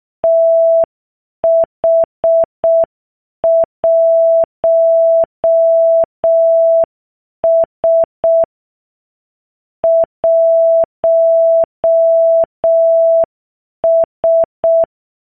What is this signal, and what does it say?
Morse code, heard as a keyed tone, about 6 wpm, 660 Hz. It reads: TH1S 1S